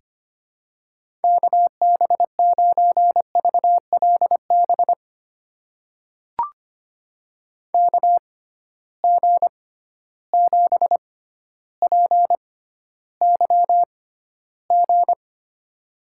Morse code, heard as a keyed tone, about 25 wpm, spaced as for 13 wpm, 700 Hz. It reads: KB9VLB KG7PYG
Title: USA Call Signs - 6 Characters - 6x Character Spacing - 25wpm